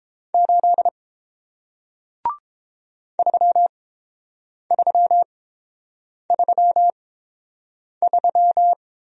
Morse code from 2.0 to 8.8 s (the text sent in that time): 3333